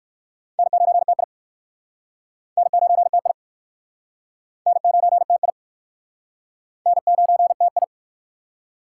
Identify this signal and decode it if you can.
Morse code, heard as a keyed tone, about 63 wpm, 700 Hz. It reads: N9TI N9TI N9TI N9TI